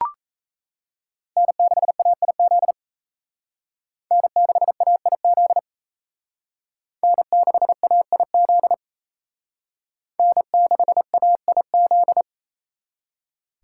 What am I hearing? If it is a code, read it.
Morse code, heard as a keyed tone, about 35 wpm, 700 Hz. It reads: N6AIZ N6AIZ N6AIZ N6AIZ